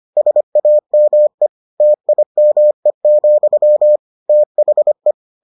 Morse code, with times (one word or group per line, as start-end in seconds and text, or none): 0.2-1.5 SAME
1.8-4.0 TIME,
4.3-5.1 THE